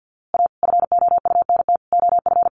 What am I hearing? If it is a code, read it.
Morse code, heard as a keyed tone, about 50 wpm, 700 Hz. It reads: A FOUNT OF
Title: A Sound Of Thunder 50wpm